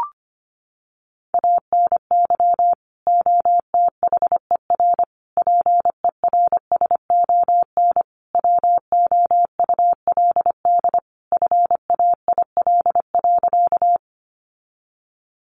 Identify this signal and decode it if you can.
Morse code, heard as a keyed tone, about 25 wpm, 700 Hz. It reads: ANY OTHER PERSON WOULD FAIL.
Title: Sentences from Top 1100 Words 25wpm